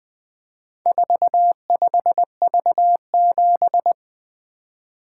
Morse code, heard as a keyed tone, about 20 wpm, 700 Hz. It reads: 45V7
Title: Four Letter-Number Combinations - 20wpm